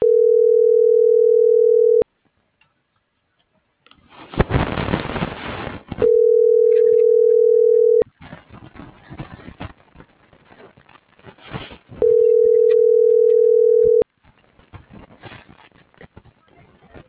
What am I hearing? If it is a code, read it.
Ringback tone (North America), 440 + 480 Hz.